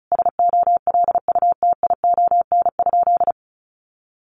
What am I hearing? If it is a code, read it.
Morse code, heard as a keyed tone, about 34 wpm, 700 Hz. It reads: SOLUTION?
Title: Sentences from Top 1300 Words 35wpm